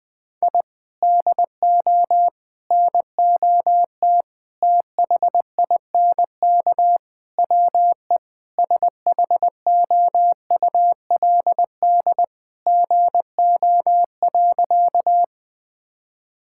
Morse code, heard as a keyed tone, about 21 wpm, 700 Hz. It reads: I DO NOT THINK WE SHOULD GO.
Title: Sentences with 4-Word Phrases - Spoken Beginning - 20wpm